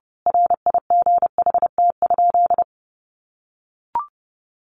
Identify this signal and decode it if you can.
Morse code, heard as a keyed tone, about 30 wpm, 700 Hz. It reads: RIGHT?
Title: A Sound Of Thunder 30wpm